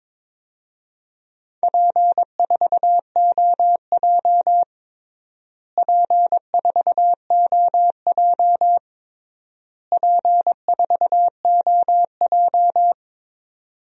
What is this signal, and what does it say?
Morse code, heard as a keyed tone, about 21 wpm, 700 Hz. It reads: P4OJ P4OJ P4OJ
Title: DXCC-100 Call Signs - 4 Characters - Repeated 3x - 22wpm